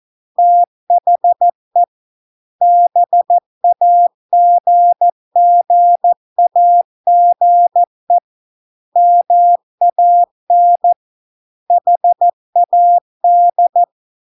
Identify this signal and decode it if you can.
Morse code, heard as a keyed tone, about 14 wpm, 700 Hz. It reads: THE BAGGAGE MAN HAD